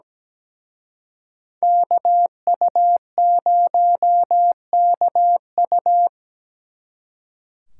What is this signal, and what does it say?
Morse code, heard as a keyed tone, about 20 wpm, 700 Hz. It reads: KU0KU
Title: Speed Racing - USA 2x2 Call Signs 17wpm